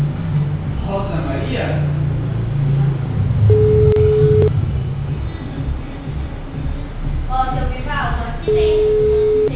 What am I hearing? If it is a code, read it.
Ringback tone (Europe, 425 Hz), with speech.
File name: out- -2020- - - .8610